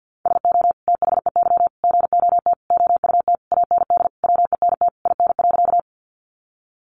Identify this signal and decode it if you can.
Morse code, heard as a keyed tone, about 50 wpm, 700 Hz. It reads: SO THEY GOT OUT AND WENT IN.
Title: Sentences from Top 300 Words 50wpm